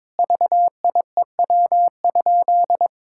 Morse code, heard as a keyed tone, about 22 wpm, 700 Hz. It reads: VIEW?
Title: Sentences from Top 1000 Words - Review Part 6 - 22wpm